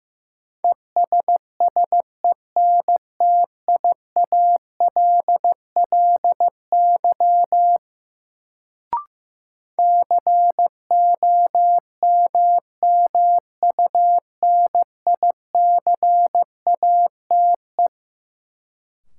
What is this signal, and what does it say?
Morse code, heard as a keyed tone, about 15 wpm, 700 Hz. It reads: ESSENTIALLY COMMUNICATE
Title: Sentences from Top 1300 Words - New Word Drill Alt - 15wpm